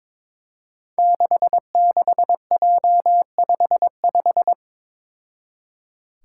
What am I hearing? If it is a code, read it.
Morse code, heard as a keyed tone, about 22 wpm, 700 Hz. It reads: 66J55
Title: Five Challenging Letter-Number Combinations 22wpm